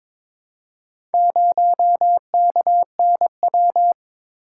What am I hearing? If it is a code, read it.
Morse code, heard as a keyed tone, about 23 wpm, 700 Hz. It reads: 0KNW
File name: practice / 22wpm / Four Letter-Number Combinations - 22wpm